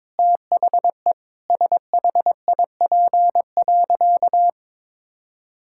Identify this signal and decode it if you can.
Morse code, heard as a keyed tone, about 22 wpm, 700 Hz. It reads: THE SHIP.
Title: Sentences from Top 1000 Words - Review Part 5 - 22wpm